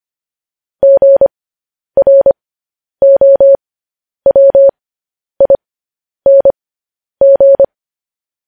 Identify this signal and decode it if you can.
Morse code, heard as a keyed tone, about 25 wpm, spaced as for 10 wpm, 550 Hz. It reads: GROWING
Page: News Headlines in Morse Code at 10 WPM